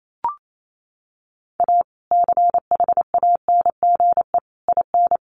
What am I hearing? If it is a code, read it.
Morse code, heard as a keyed tone, about 28 wpm, 700 Hz. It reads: A CHANGE IN
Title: Sentences from Top 1500 Words 28wpm